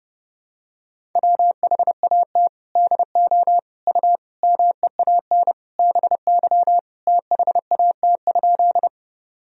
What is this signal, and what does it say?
Morse code, heard as a keyed tone, about 32 wpm, 700 Hz. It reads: WHAT DO U MEAN BY THAT?
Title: Sentences from Top 200 Words and Abbreviations 30wpm